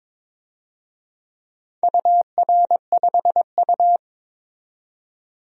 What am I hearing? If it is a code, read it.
Morse code, heard as a keyed tone, about 22 wpm, 700 Hz. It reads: UR5U
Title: DXCC-100 Call Signs - 4 Characters - 22wpm